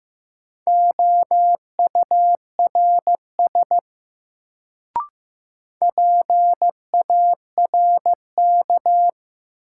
Morse code, heard as a keyed tone, about 15 wpm, 700 Hz. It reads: OURS PARK
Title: Sentences from Top 1300 Words - New Word Drill Alt - 15wpm